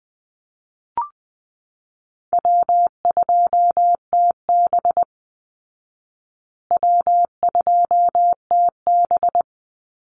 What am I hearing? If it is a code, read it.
Morse code, heard as a keyed tone, about 20 wpm, 700 Hz. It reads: W2TB W2TB
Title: Call Signs Repeated 3x - 20wpm